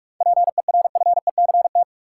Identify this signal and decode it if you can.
Morse code, heard as a keyed tone, about 46 wpm, 700 Hz. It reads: PERFECT
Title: Sentences from Top 1000 Words - New Word Drill - 45wpm